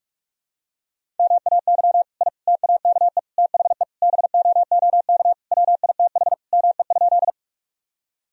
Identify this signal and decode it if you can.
Morse code, heard as a keyed tone, about 43 wpm, 700 Hz. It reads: MAY I TAKE THE BOOK WITH ME?
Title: Sentences from Top 1000 Words - Review Part 1 - 45wpm